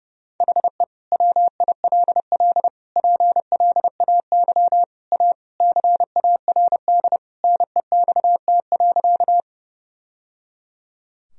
Morse code, heard as a keyed tone, about 30 wpm, 700 Hz. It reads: HE WILL PLAY A CARD NEXT.
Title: Sentences from Top 1000 Words - Review Part 4 - 30wpm